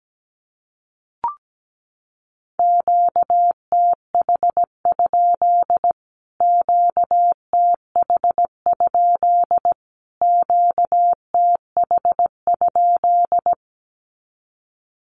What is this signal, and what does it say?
Morse code, heard as a keyed tone, about 17 wpm, 700 Hz. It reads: QTH? QTH? QTH?
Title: Q-Codes Repeated 3x 17wpm